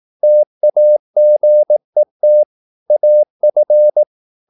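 Morse code, heard as a keyed tone, about 18 wpm, 600 Hz. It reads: TAGET AF
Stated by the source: Slow (QRS) Morse practice at 18 wpm, CW med Gnister.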